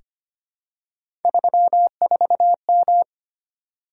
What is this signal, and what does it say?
Morse code, heard as a keyed tone, about 25 wpm, 700 Hz. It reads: 34M
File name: Three Letter-Number Combinations - 25wpm